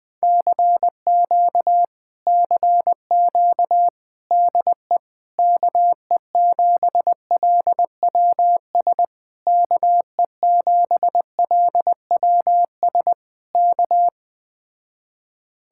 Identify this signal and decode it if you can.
Morse code, heard as a keyed tone, about 20 wpm, 700 Hz. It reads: CQ CQ DE KE7LWS KE7LWS K